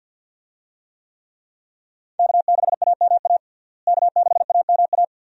Morse code, heard as a keyed tone, about 48 wpm, 700 Hz. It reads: K6AMA K6AMA
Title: USA Call Signs - 5 Characters - Repeated 2x - 50wpm